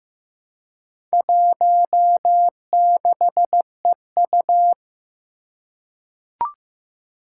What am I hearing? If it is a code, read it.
Morse code, heard as a keyed tone, about 15 wpm, 700 Hz. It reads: 16EU
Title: Four Challenging Letter-Number Combinations 15wpm